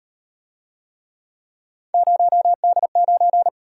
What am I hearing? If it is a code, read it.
Morse code, heard as a keyed tone, about 34 wpm, 700 Hz. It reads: 0D9